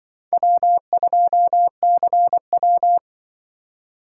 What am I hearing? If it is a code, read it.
Morse code, heard as a keyed tone, about 24 wpm, 700 Hz. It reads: W2CW
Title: Speed Racing - USA 1x2 Call Signs 20wpm